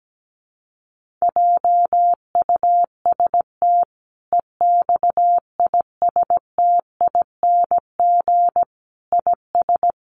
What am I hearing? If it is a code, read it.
Morse code, heard as a keyed tone, about 17 wpm, 700 Hz. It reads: JUST EXISTING IS
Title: Sentences from Top 1500 Words 17wpm